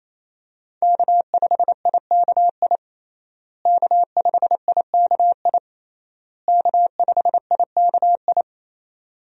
Morse code, heard as a keyed tone, about 28 wpm, 700 Hz. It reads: K5IKI K5IKI K5IKI
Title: USA Call Signs - 5 Characters - Repeated 3x - 28wpm